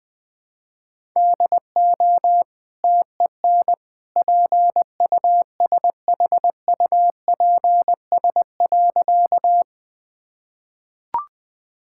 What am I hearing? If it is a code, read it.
Morse code, heard as a keyed tone, about 20 wpm, 700 Hz. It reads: DO TEN PUSHUPS.
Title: Easy Sentences with Hints - 20wpm